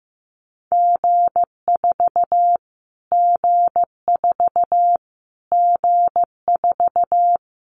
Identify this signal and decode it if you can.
Morse code, heard as a keyed tone, about 15 wpm, 700 Hz. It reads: G4 G4 G4
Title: Callsign Prefixes for Top 25 DXCC - Repeated 3x - 15wpm